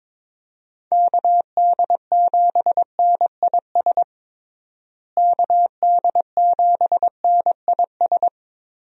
Morse code, heard as a keyed tone, about 22 wpm, 700 Hz. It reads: KD7NIS KD7NIS
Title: USA Call Signs - 6 Characters - Repeated 2x - 22wpm